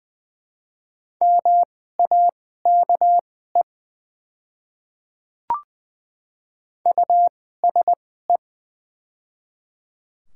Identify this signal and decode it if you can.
Morse code, heard as a keyed tone, about 20 wpm, spaced as for 14 wpm, 700 Hz. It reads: MAKE USE
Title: 2x Character Spacing - Top 100 Words - 20wpm